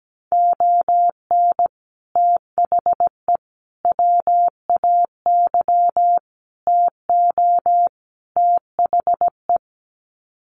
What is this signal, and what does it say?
Morse code, heard as a keyed tone, about 17 wpm, 700 Hz. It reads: ON THE WAY TO THE
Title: Sets of 5 Words 17wpm